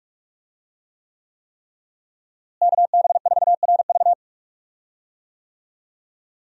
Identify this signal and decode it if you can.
Morse code, heard as a keyed tone, about 42 wpm, 700 Hz. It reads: KB4RV
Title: USA Call Signs - 5 Characters - Repeated 3x - 45wpm